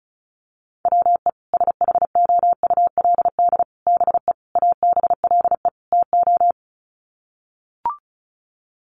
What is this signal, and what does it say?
Morse code, heard as a keyed tone, about 34 wpm, 700 Hz. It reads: WE SHOULD BE ABLE TO
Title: Sets of 5 Words Encore2 35wpm